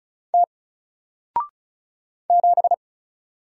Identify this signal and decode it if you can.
Morse code, heard as a keyed tone, about 36 wpm, 700 Hz. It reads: T 7
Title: Single Letter-Number Alt 35wpm